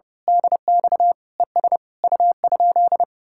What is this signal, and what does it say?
Morse code, heard as a keyed tone, about 28 wpm, 700 Hz. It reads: DX ES U?
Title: Speed-Racing Common QSO Phrases - 30wpm